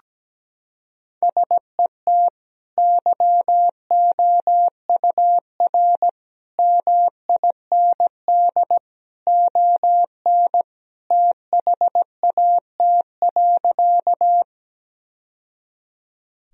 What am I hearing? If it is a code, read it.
Morse code, heard as a keyed tone, about 17 wpm, 700 Hz. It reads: SET YOUR MIND ON THAT.
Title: Sentences from Top 1000 Words - Review Part 1 - 17wpm